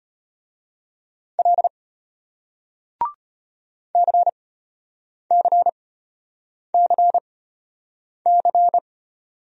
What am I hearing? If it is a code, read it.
Morse code, heard as a keyed tone, about 34 wpm, spaced as for 8 wpm, 700 Hz. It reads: L CCCC